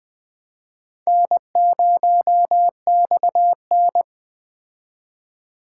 Morse code, heard as a keyed tone, about 21 wpm, 700 Hz. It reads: N0XN
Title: Speed Racing - USA 1x2 Call Signs 17wpm